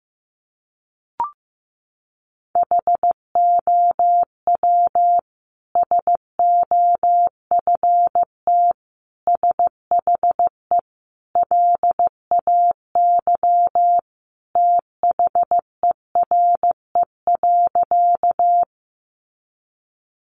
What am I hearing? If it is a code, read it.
Morse code, heard as a keyed tone, about 15 wpm, 700 Hz. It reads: HOW SOFT SHE LAY THERE.
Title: Sentences from Top 1500 Words 15wpm